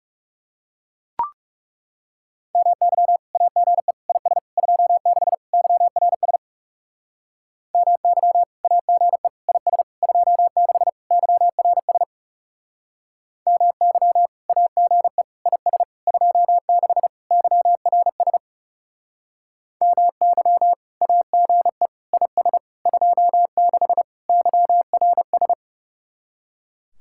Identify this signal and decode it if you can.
Morse code, heard as a keyed tone, about 36 wpm, 700 Hz. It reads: MY AGE IS 26 YRS MY AGE IS 26 YRS MY AGE IS 26 YRS MY AGE IS 26 YRS
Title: Speed-Racing Common QSO Phrases Encore - 30wpm